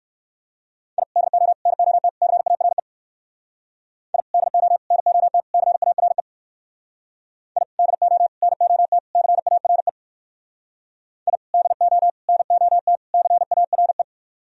Morse code, heard as a keyed tone, about 63 wpm, 700 Hz. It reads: I DO NOT CARE I DO NOT CARE I DO NOT CARE I DO NOT CARE